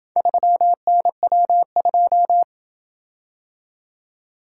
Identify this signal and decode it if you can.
Morse code, heard as a keyed tone, about 27 wpm, 700 Hz. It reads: 3NW2